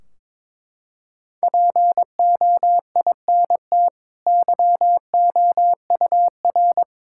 Morse code, heard as a keyed tone, about 22 wpm, 700 Hz. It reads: POINT YOUR
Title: Sentences from Top 1600 Words 22wpm